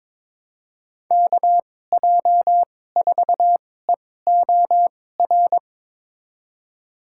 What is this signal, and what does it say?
Morse code, heard as a keyed tone, about 22 wpm, spaced as for 16 wpm, 700 Hz. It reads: KJ4EOR
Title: USA Call Signs - 6 Characters - 2x Character Spacing - 22wpm